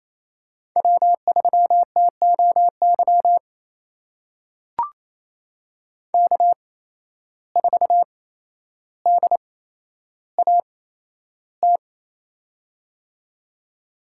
Morse code, heard as a keyed tone, about 28 wpm, spaced as for 12 wpm, 700 Hz. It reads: W3TOY K4DAT
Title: USA Call Signs - 5 Characters - 8x Character Spacing - 28wpm